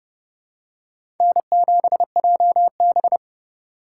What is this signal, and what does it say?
Morse code, heard as a keyed tone, about 30 wpm, 700 Hz. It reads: N7JB
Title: USA Call Signs - 4 Characters - 3x Character Spacing - 30wpm